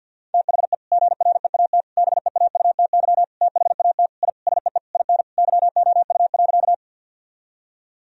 Morse code, heard as a keyed tone, about 50 wpm, 700 Hz. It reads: THE GREAT BEAUTY THAT I SEE IN YOU.
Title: Sentences from Top 1000 Words - Review Part 2 - 50wpm